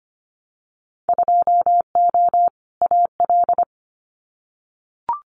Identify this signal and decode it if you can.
Morse code, heard as a keyed tone, about 25 wpm, 700 Hz. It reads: 2O AL
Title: Winter Field Day Exchanges 25wpm